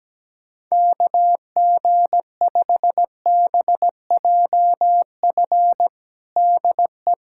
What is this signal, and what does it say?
Morse code, heard as a keyed tone, about 17 wpm, 700 Hz. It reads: KG5BJF DE
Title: USA Callsign DE Callsign - 6 Characters - 17wpm